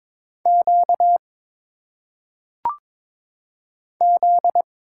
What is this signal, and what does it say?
Morse code, heard as a keyed tone, about 22 wpm, 700 Hz. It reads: Q Z
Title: Single Letters - Phonetic Pronunciation 22wpm